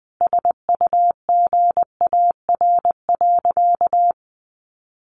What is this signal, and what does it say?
Morse code, heard as a keyed tone, about 20 wpm, 700 Hz. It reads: SUGAR.
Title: Sentences from Top 1200 Words 20wpm